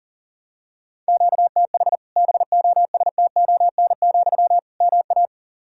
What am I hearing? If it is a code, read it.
Morse code, heard as a keyed tone, about 40 wpm, 700 Hz. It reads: QTH BOSTON, MA
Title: Practice QSO - Exchange 1 - Beginning - 40wpm